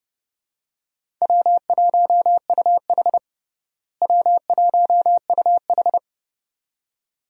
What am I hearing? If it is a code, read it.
Morse code, heard as a keyed tone, about 32 wpm, 700 Hz. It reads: W1UH W1UH
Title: USA Call Signs - 4 Characters - Repeated 2x - 30wpm